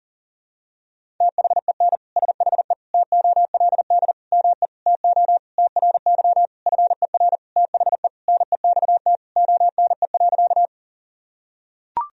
Morse code, heard as a keyed tone, about 40 wpm, 700 Hz. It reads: THEN SHE TOLD ME TO TRY FER THE NEXT ONE.
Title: Sentences from Top 400 Words and Abbreviations 40wpm